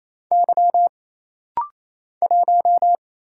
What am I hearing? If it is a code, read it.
Morse code, heard as a keyed tone, about 27 wpm, 700 Hz. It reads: Y 1
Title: Single Letter-Number Alt - Rapid-Fire 28wpm